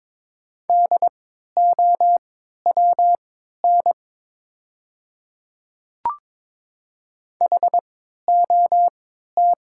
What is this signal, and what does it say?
Morse code, heard as a keyed tone, about 22 wpm, spaced as for 12 wpm, 700 Hz. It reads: DOWN HOT
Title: 3x Character Spacing - Top 100 Words - 22wpm